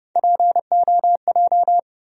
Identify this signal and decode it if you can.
Morse code, heard as a keyed tone, about 31 wpm, 700 Hz. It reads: POJ
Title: USA Call Sign Postfixes - 3 Letters 30wpm